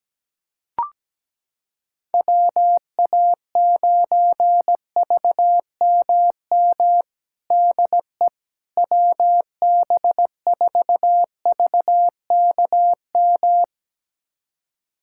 Morse code, heard as a keyed tone, about 17 wpm, 700 Hz. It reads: WA9VMM DE WB4VKM
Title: USA Callsign DE Callsign - 6 Characters - 17wpm